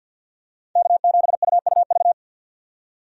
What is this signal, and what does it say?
Morse code, heard as a keyed tone, about 49 wpm, 700 Hz. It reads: K7UUV
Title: USA Call Signs - 5 Characters - 50wpm